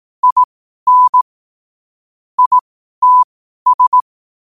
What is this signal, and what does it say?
Morse code, heard as a keyed tone, about 18 wpm, spaced as for 12 wpm, 1000 Hz. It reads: IN ITS
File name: qotdfiles / 12wpm